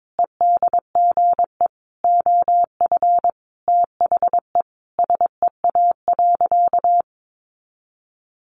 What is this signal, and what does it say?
Morse code, heard as a keyed tone, about 22 wpm, 700 Hz. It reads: EDGE OF THE SEA.
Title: Sentences from Top 1300 Words 22wpm